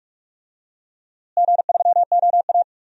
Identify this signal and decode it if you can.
Morse code, heard as a keyed tone, about 46 wpm, 700 Hz. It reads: G3OA
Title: DXCC-100 Call Signs - 4 Characters - Repeated 2x - 45wpm